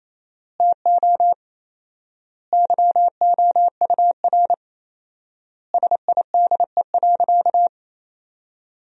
Morse code, heard as a keyed tone, about 28 wpm, 700 Hz. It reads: TO YOUR SIDE.
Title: 4x Word-Spacing - Sentences from Top 400 Words - 28wpm